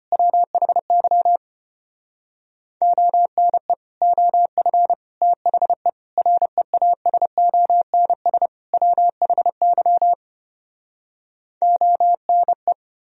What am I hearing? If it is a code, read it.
Morse code, heard as a keyed tone, about 33 wpm, 700 Hz. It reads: WHY ONE OF THE REASONS WHY ONE